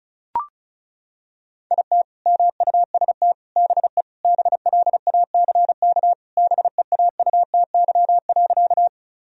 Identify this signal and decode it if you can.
Morse code, heard as a keyed tone, about 36 wpm, 700 Hz. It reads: IT MUST BE BLACK BEAUTY.
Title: Sentences from Top 1000 Words - Review Part 2 - 35wpm